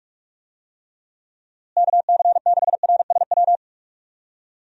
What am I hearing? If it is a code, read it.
Morse code, heard as a keyed tone, about 42 wpm, 700 Hz. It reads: KC6RSW
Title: USA Call Signs - 6 Characters - Repeated 3x - 45wpm